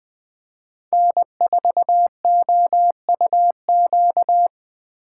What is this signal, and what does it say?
Morse code, heard as a keyed tone, about 20 wpm, 700 Hz. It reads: N4OUQ
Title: USA Call Signs - 5 Characters - 4x Character Spacing - 20wpm